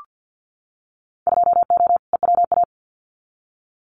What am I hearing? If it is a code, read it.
Morse code, heard as a keyed tone, about 50 wpm, 700 Hz. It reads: 2O EWA
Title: Winter Field Day Exchanges Encore 50wpm